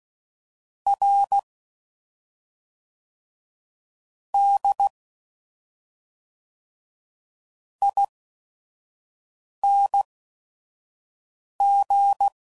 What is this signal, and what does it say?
Morse code, heard as a keyed tone, about 16 wpm, spaced as for 5 wpm, 800 Hz. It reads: R D ING
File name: Suffixes / Special ING's / SlowToFaster ING's